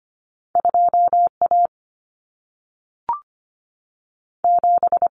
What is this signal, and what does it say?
Morse code, heard as a keyed tone, about 25 wpm, 700 Hz. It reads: 2 A 7
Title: Common Cut Numbers - 2-Digit 25wpm